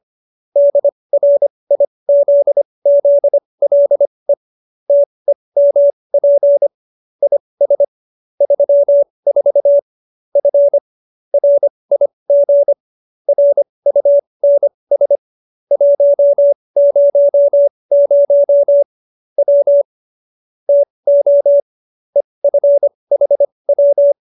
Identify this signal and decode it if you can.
Morse code, heard as a keyed tone, about 25 wpm, spaced as for 20 wpm, 570 Hz. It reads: DRIZZLE TEMP IS 34 F RIG RUNS 100 W TO EFHW